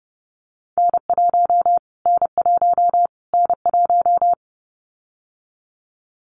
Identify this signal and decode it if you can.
Morse code, heard as a keyed tone, about 30 wpm, 700 Hz. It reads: N1 N1 N1
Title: Callsign Prefixes for Top 25 DXCC - Repeated 3x - 30wpm